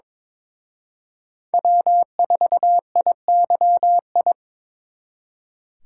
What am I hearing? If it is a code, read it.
Morse code, heard as a keyed tone, about 26 wpm, 700 Hz. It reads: W4IYI
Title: Speed Racing - USA Call Signs 22wpm